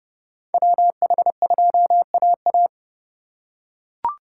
Morse code, heard as a keyed tone, about 30 wpm, 700 Hz. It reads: WH2AA
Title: USA Call Signs - 5 Characters - 4x Character Spacing - 30wpm